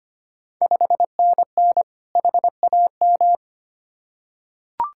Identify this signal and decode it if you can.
Morse code, heard as a keyed tone, about 25 wpm, 700 Hz. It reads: 5NN HAM
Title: Ontario QSO Party Exchange 25wpm